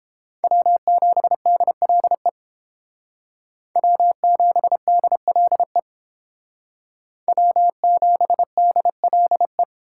Morse code, heard as a keyed tone, about 29 wpm, 700 Hz. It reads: W7DLE W7DLE W7DLE